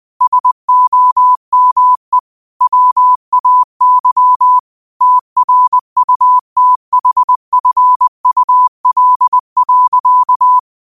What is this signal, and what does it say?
Morse code, heard as a keyed tone, about 20 wpm, 1000 Hz. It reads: SOME WAY TRUTHFUL.